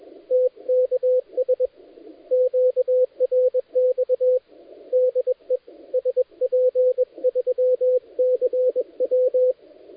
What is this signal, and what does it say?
Morse code, heard as a keyed tone, about 21 wpm, 510 Hz, with radio noise. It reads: TKS QRX DE SP3CW
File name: dlazaawansowanych / nagraniazpasma / titawaNETcontrol